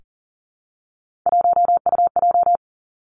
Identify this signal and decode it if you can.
Morse code, heard as a keyed tone, about 42 wpm, 700 Hz. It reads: 1UJ
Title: Three Challenging Letter-Number Combinations 40wpm